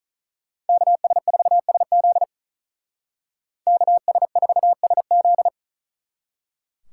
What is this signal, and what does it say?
Morse code, heard as a keyed tone, about 46 wpm, 700 Hz. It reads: KS4SZ KS4SZ